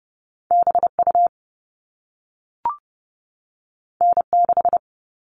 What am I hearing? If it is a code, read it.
Morse code, heard as a keyed tone, about 30 wpm, 700 Hz. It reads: BU N6
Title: All Cut Numbers - 2-Digit 30wpm